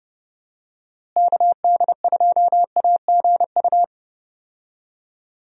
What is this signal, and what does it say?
Morse code, heard as a keyed tone, about 30 wpm, 700 Hz. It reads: KD2AGU
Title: USA Call Signs - 6 Characters - Repeated 2x - 30wpm